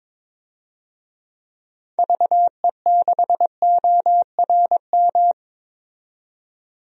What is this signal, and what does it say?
Morse code, heard as a keyed tone, about 21 wpm, 700 Hz. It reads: VE6ORM
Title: DXCC-100 Call Signs - 6 Characters - 22wpm